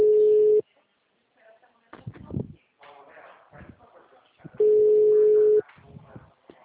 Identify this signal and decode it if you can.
Ringback tone (Europe, 425 Hz), with speech.